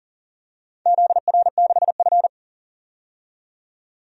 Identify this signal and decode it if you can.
Morse code, heard as a keyed tone, about 40 wpm, 690 Hz. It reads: ZR6F
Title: DXCC-100 Call Signs - 4 Characters - 40wpm